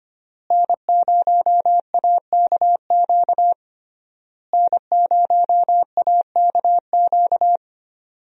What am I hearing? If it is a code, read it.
Morse code, heard as a keyed tone, about 25 wpm, 700 Hz. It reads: N0AKQ N0AKQ
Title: USA Call Signs - 5 Characters - Repeated 3x - 25wpm